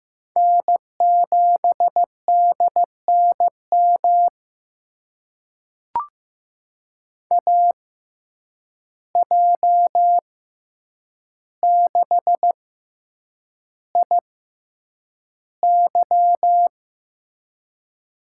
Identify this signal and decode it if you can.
Morse code, heard as a keyed tone, about 15 wpm, spaced as for 8 wpm, 700 Hz. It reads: N7DNM AJ6IY
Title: USA Call Signs - 5 Characters - 6x Character Spacing - 15wpm